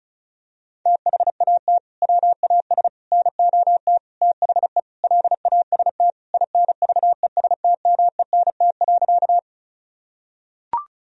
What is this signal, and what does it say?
Morse code, heard as a keyed tone, about 36 wpm, 700 Hz. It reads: THAT WAS NOT THE LAST INVESTMENT.
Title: Sentences from Top 1000 Words - Review Part 4 - 35wpm